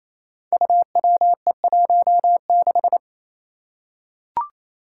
Morse code, heard as a keyed tone, about 28 wpm, 700 Hz. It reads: UWE16
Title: Five Challenging Letter-Number Combinations 28wpm